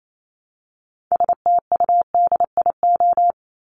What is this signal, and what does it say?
Morse code, heard as a keyed tone, about 28 wpm, 700 Hz. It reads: STUDIO
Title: Top 1001-1250 Words 28wpm